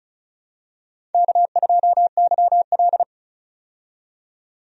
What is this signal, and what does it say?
Morse code, heard as a keyed tone, about 34 wpm, 700 Hz. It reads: K2YL
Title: USA Call Signs - 4 Characters - 35wpm